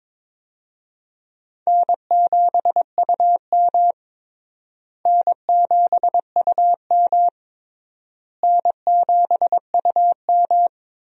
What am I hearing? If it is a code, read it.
Morse code, heard as a keyed tone, about 22 wpm, 700 Hz. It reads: N7UM N7UM N7UM
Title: USA Call Signs - 4 Characters - Repeated 3x - 22wpm